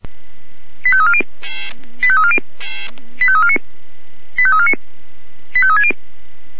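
Five-tone selective call (ZVEI), heard as quick runs of five tones, tones of about 70 ms. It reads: 85340 85340 85349 85349 85360
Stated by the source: ZVEI five-tone selective call 85349 received by SDR.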